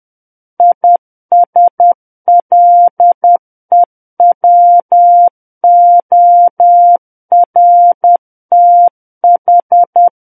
Morse code, heard as a keyed tone, about 10 wpm, 700 Hz. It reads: ISLEWORTH
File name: Book One / 01-Chapter 1 - 10 WPM